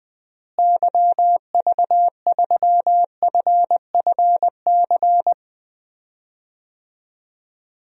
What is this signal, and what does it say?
Morse code, heard as a keyed tone, about 20 wpm, 700 Hz. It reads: YV3FFC
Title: DXCC-100 Call Signs - 6 Characters - Repeated 3x - 20wpm